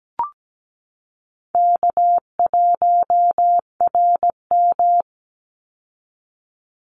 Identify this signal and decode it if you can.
Morse code, heard as a keyed tone, about 17 wpm, 700 Hz. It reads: K1RM
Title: Call Signs Repeated 2x - 17wpm